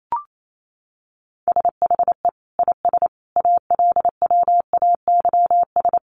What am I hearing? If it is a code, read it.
Morse code, heard as a keyed tone, about 28 wpm, 700 Hz. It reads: SHE IS ALWAYS
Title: Sentences from Top 1200 Words 28wpm